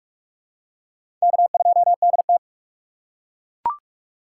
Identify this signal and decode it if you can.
Morse code, heard as a keyed tone, about 42 wpm, 700 Hz. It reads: K2DT
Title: USA Call Signs - 2x Character Spacing - 45wpm